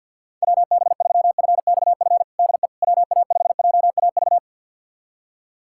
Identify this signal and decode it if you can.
Morse code, heard as a keyed tone, about 50 wpm, 700 Hz. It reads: WB3FXF DE WA5JAV
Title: USA Callsign DE Callsign 50wpm